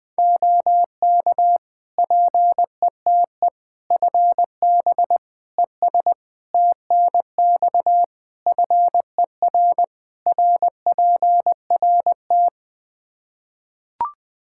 Morse code, heard as a keyed tone, about 20 wpm, 700 Hz. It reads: OK PETE FB ES TNX FER RPRT
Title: Practice QSO - Exchange 2 - Info - 20wpm